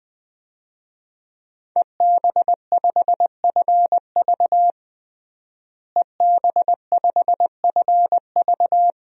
Morse code, heard as a keyed tone, about 20 wpm, 700 Hz. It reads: EB5FV EB5FV
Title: DXCC-100 Call Signs - 5 Characters - Repeated 3x - 20wpm